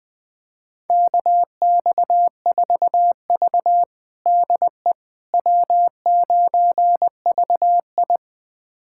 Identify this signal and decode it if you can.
Morse code, heard as a keyed tone, about 20 wpm, 700 Hz. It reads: KX4V DE W9VI
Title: USA Callsign DE Callsign - 4 Characters - 20wpm